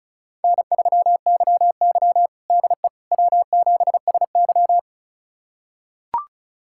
Morse code, heard as a keyed tone, about 36 wpm, 700 Hz. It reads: N3YY DE W7SY
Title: USA Callsign DE Callsign - 4 Characters - 35wpm